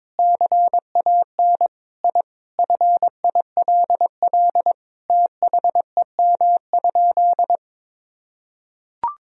Morse code, heard as a keyed tone, about 21 wpm, 700 Hz. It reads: CAN I FILL THEM?
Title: Sentences from Top 1000 Words - Review Part 3 - 22wpm